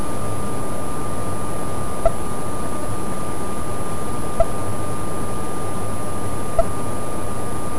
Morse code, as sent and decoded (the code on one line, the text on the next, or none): none